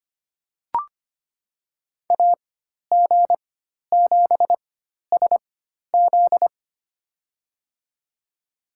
Morse code, heard as a keyed tone, about 25 wpm, spaced as for 12 wpm, 700 Hz. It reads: AG7SZ
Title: USA Call Signs - 4x Character Spacing - 25wpm